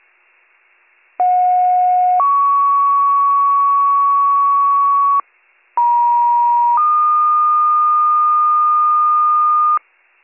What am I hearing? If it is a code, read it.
Two-tone page, heard as a one-second tone, then a three-second tone, about 730 then 1120 Hz, about 950 then 1220 Hz. It is heard from 1.2 to 9.8 s.